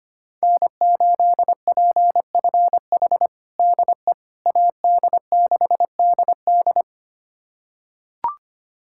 Morse code, heard as a keyed tone, about 25 wpm, 700 Hz. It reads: N8PFH DE AD6DD
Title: USA Callsign DE Callsign - 5 Characters - 25wpm